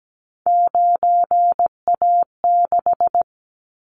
Morse code, heard as a keyed tone, about 17 wpm, 700 Hz. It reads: 9A6
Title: Callsign Prefixes for Top 100 DXCC - 17wpm